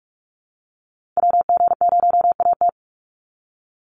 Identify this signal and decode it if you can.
Morse code, heard as a keyed tone, about 44 wpm, 700 Hz. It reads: WG0AT
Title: NA SOTA Activator Callsigns 45wpm